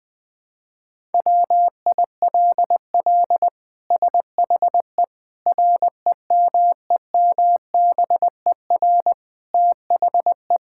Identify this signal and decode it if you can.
Morse code, heard as a keyed tone, about 20 wpm, 700 Hz. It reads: WILL SHE REMEMBER THE
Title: Sentences from Top 1000 Words - Review Part 6 - 20wpm